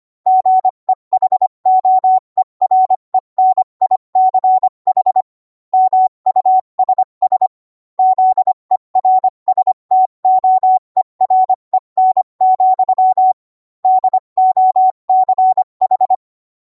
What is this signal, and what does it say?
Morse code, heard as a keyed tone, about 25 wpm, 750 Hz. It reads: GEHOEREN ICH MUSS ZERSTOEREN, DOCH